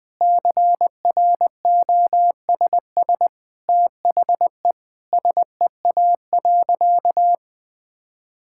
Morse code, heard as a keyed tone, about 19 wpm, 700 Hz. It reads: CROSS THE SEA.
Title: Sentences from Top 1000 Words - Review Part 6 - 20wpm